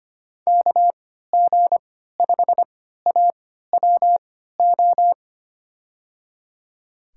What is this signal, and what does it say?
Morse code, heard as a keyed tone, about 25 wpm, spaced as for 14 wpm, 700 Hz. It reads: KG5AWO
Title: USA Call Signs - 6 Characters - 3x Character Spacing - 25wpm